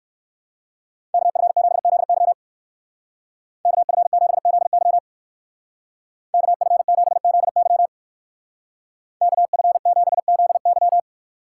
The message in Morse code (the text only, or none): KF7ZY KF7ZY KF7ZY KF7ZY